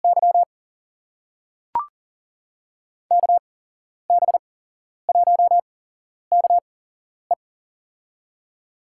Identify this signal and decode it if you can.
Morse code, heard as a keyed tone, about 40 wpm, spaced as for 11 wpm, 700 Hz. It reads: Y KB1KE
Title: USA Call Signs - 5 Characters - 8x Character Spacing - 40wpm